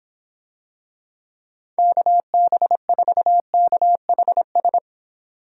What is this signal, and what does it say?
Morse code, heard as a keyed tone, about 25 wpm, 700 Hz. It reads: KB4KHS